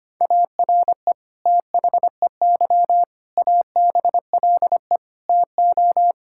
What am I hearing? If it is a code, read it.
Morse code, heard as a keyed tone, about 25 wpm, 700 Hz. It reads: ARE THEY ABLE TO
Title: Sentences from Top 1000 Words - Review Part 5 - 25wpm